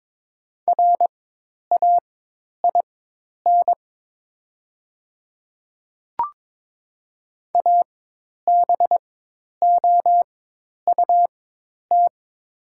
Morse code, heard as a keyed tone, about 22 wpm, spaced as for 10 wpm, 690 Hz. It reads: RAIN ABOUT
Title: 4x Character Spacing - Top 300 Words - 22wpm